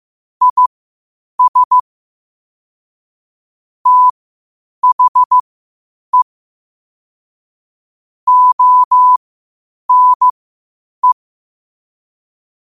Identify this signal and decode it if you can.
Morse code, heard as a keyed tone, about 15 wpm, spaced as for 8 wpm, 1000 Hz. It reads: IS THE ONE
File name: qotdfiles / 8wpm